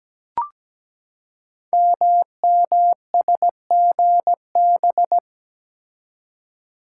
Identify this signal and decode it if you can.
Morse code, heard as a keyed tone, about 17 wpm, 700 Hz. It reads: MMSGB
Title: Code Groups - 5 Letters - 17wpm